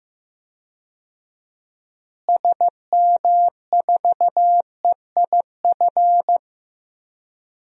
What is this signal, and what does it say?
Morse code, heard as a keyed tone, about 15 wpm, 700 Hz. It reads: SM4EIF
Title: DXCC-100 Call Signs - 6 Characters - Repeated 3x - 15wpm